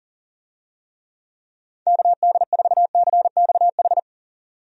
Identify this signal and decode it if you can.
Morse code, heard as a keyed tone, about 40 wpm, 700 Hz. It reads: KD4CXH
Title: USA Call Signs - 6 Characters - 40wpm